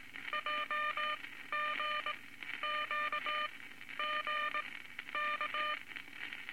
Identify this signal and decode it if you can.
Morse code, heard as a keyed tone, about 19 wpm, spaced as for 12 wpm, 1200 Hz, with noise. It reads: JGQGK
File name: Morse Code Records / 19 A LCM 166 Ten Words Per Minute